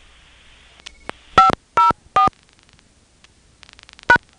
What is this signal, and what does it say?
DTMF, heard as telephone touch tones, tones of about 120 ms. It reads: 504 #
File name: phonetrips / Nantes1HQ